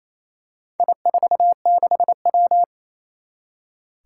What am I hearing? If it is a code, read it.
Morse code, heard as a keyed tone, about 28 wpm, 700 Hz. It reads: I46W
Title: Four Challenging Letter-Number Combinations 28wpm